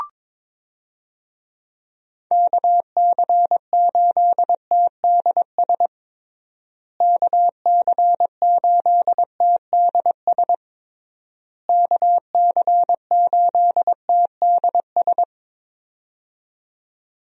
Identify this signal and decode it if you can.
Morse code, heard as a keyed tone, about 22 wpm, 700 Hz. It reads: KC8TDS KC8TDS KC8TDS
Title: USA Call Signs - 6 Characters - Repeated 3x - 22wpm